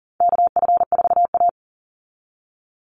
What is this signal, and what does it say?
Morse code, heard as a keyed tone, about 40 wpm, 700 Hz. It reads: KF4A
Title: Call Signs Repeated 2x - 40wpm